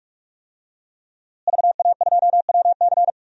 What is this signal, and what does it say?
Morse code, heard as a keyed tone, about 46 wpm, 700 Hz. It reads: VA2WC
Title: DXCC-100 Call Signs - 5 Characters - Repeated 3x - 45wpm